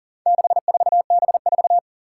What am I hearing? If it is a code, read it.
Morse code, heard as a keyed tone, about 38 wpm, 700 Hz. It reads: 64B4
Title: Four Challenging Letter-Number Combinations 40wpm